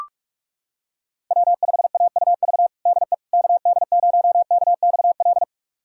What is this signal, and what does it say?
Morse code, heard as a keyed tone, about 42 wpm, 700 Hz. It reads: W5AUV DE KD0KXL
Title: USA Callsign DE Callsign 45wpm